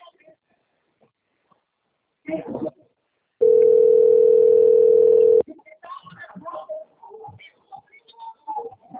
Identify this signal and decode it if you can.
Ringback tone (North America), 440 + 480 Hz, with speech.
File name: rg-1600-unknown- - - .20007